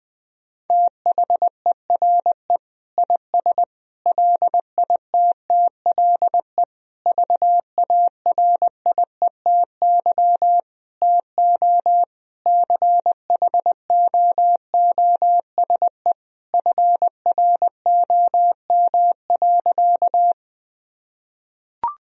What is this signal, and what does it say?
Morse code, heard as a keyed tone, about 21 wpm, 700 Hz. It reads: THERE IS LITTLE VARIETY TO CHOOSE FROM.
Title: Sentences from Top 1000 Words - Review Part 4 - 20wpm